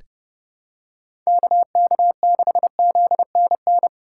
Morse code, heard as a keyed tone, about 32 wpm, 700 Hz. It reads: KK6ZNN
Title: USA Call Signs - 6 Characters - 3x Character Spacing - 30wpm